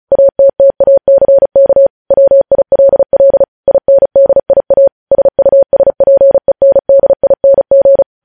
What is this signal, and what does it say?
Morse code, heard as a keyed tone, about 36 wpm, 550 Hz. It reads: ATTACK WILL INDIA SUSPENDING
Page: News headlines in Morse code at 35 wpm